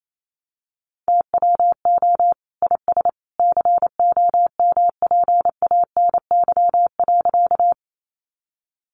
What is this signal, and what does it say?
Morse code, heard as a keyed tone, about 28 wpm, 700 Hz. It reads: TWO IS COMPANY.
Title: Sentences from Top 600 Words 28wpm